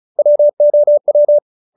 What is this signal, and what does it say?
Morse code, heard as a keyed tone, about 35 wpm, 590 Hz. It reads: WOW